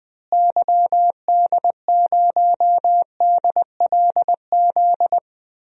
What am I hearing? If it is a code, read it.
Morse code, heard as a keyed tone, about 20 wpm, 700 Hz. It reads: YD0DLZ
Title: DXCC-100 Call Signs - 6 Characters - Repeated 3x - 20wpm